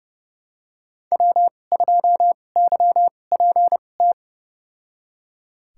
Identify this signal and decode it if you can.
Morse code, heard as a keyed tone, about 31 wpm, spaced as for 22 wpm, 700 Hz. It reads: W2YPT
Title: USA Call Signs - 2x Character Spacing - 30wpm